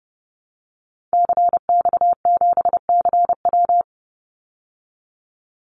Morse code, heard as a keyed tone, about 30 wpm, 700 Hz. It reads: CX7CW
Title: Call Signs Repeated 3x - 30wpm